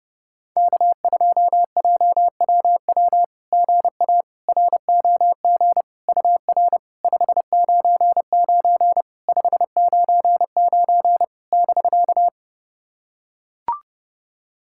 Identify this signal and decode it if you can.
Morse code, heard as a keyed tone, about 30 wpm, 690 Hz. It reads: K2JWW GA ROG UR 599 599 <BK>
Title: SOTA QSO Practice 30wpm